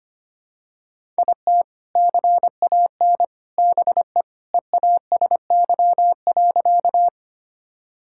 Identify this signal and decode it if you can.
Morse code, heard as a keyed tone, about 25 wpm, 700 Hz. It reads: IT CAN BE EASY.
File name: practice / 25wpm / Sentences with 2-Word Phrases - 25wpm